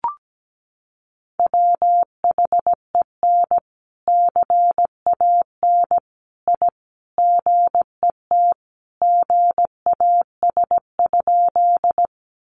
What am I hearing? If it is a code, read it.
Morse code, heard as a keyed tone, about 17 wpm, 700 Hz. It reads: WHEN CAN I GET GAS?
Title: Sentences from Top 1100 Words 17wpm